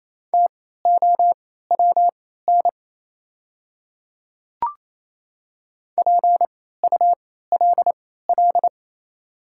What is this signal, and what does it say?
Morse code, heard as a keyed tone, about 28 wpm, spaced as for 16 wpm, 700 Hz. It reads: TOWN PULL
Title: 3x Character Spacing - Top 500 Words - 28wpm